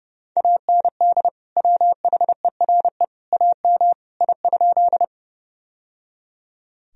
Morse code, heard as a keyed tone, about 30 wpm, 700 Hz. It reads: AND WHERE AM I?
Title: Sentences from Top 1000 Words - Review Part 2 - 30wpm